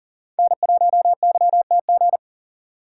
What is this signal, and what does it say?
Morse code, heard as a keyed tone, about 40 wpm, 700 Hz. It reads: N1YTG
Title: USA Call Signs - 5 Characters - 4x Character Spacing - 40wpm